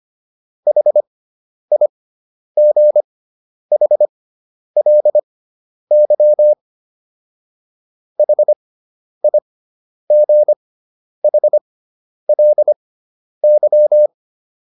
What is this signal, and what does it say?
Morse code, heard as a keyed tone, about 25 wpm, spaced as for 10 wpm, 600 Hz. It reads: HIGHLY HIGHLY